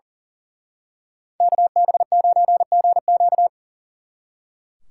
Morse code, heard as a keyed tone, about 37 wpm, 700 Hz. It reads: KB9GQ
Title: Speed Racing - USA 2x2 Call Signs 40wpm